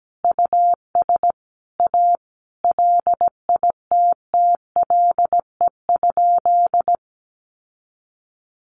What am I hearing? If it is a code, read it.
Morse code, heard as a keyed tone, about 17 wpm, 700 Hz. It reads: US A LITTLE?
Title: Sentences from Top 900 Words 17wpm